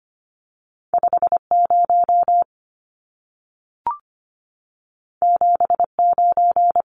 Morse code, heard as a keyed tone, about 25 wpm, 700 Hz. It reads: 50 79
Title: Numbers - 2-Digit 25wpm